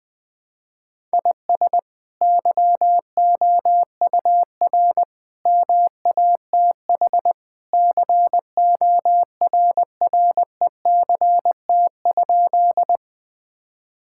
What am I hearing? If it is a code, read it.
Morse code, heard as a keyed tone, about 20 wpm, 700 Hz. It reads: IS YOUR MATH CORRECT?
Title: Sentences from Top 1000 Words - Review Part 6 - 20wpm